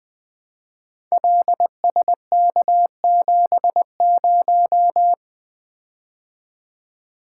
Morse code, heard as a keyed tone, about 21 wpm, 700 Hz. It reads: LSK70
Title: Code Groups - 5 Letter:Numbers - 20wpm